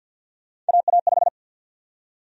Morse code, heard as a keyed tone, about 52 wpm, 700 Hz. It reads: AA5
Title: USA Call Sign Prefixes - 50wpm